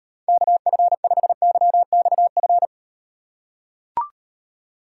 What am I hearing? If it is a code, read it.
Morse code, heard as a keyed tone, about 36 wpm, 700 Hz. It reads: KF5YXF